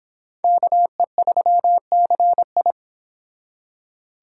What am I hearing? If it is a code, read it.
Morse code, heard as a keyed tone, about 26 wpm, 700 Hz. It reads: KE3CI